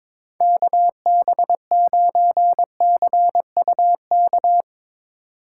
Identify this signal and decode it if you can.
Morse code, heard as a keyed tone, about 22 wpm, 700 Hz. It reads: KB9CUK
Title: USA Call Signs - 6 Characters - 4x Character Spacing - 22wpm